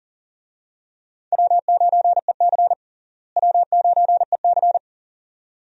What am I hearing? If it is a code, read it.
Morse code, heard as a keyed tone, about 40 wpm, 700 Hz. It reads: W9EC W9EC
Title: USA Call Signs - 4 Characters - Repeated 2x - 40wpm